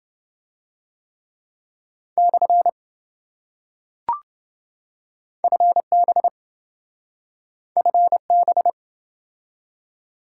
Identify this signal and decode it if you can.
Morse code, heard as a keyed tone, about 26 wpm, 700 Hz. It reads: / FB FB